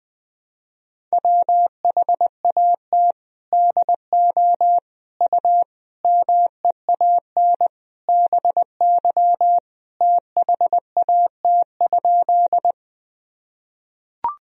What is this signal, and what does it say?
Morse code, heard as a keyed tone, about 20 wpm, 700 Hz. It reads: WHAT DO U MEAN BY THAT?
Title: Sentences from Top 200 Words and Abbreviations 20wpm